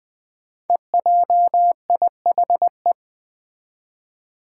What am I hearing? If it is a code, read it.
Morse code, heard as a keyed tone, about 20 wpm, 700 Hz. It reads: EJIHE
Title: Five Challenging Letter-Number Combinations 20wpm